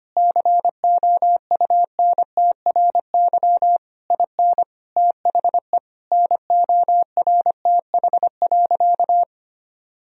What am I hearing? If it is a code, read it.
Morse code, heard as a keyed tone, about 25 wpm, 700 Hz. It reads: COUNTRY IN THE NORTH.
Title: Sentences from Top 1000 Words - Review Part 1 - 25wpm